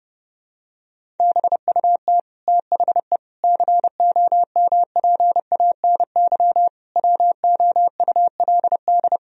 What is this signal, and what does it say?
Morse code, heard as a keyed tone, about 30 wpm, 700 Hz. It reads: BUT THE COMPANY WOULD
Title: Sentences from Top 1000 Words - Review Part 3 - 30wpm